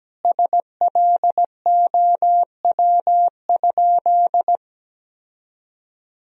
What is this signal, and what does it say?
Morse code, heard as a keyed tone, about 17 wpm, 700 Hz. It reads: SLOW?
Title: Sentences from Top 1000 Words - Review Part 2 - 17wpm